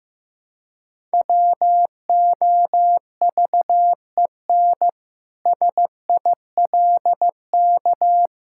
Morse code, heard as a keyed tone, about 16 wpm, 700 Hz. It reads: WOVEN SILK
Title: Pangram Challenge v2 - 15wpm